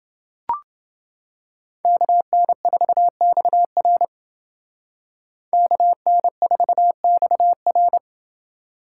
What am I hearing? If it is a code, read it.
Morse code, heard as a keyed tone, about 29 wpm, 700 Hz. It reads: KN4XR KN4XR